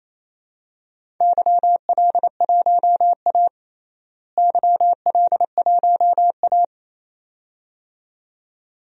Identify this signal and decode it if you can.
Morse code, heard as a keyed tone, about 28 wpm, 700 Hz. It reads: YL1A YL1A
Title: DXCC-100 Call Signs - 4 Characters - Repeated 2x - 28wpm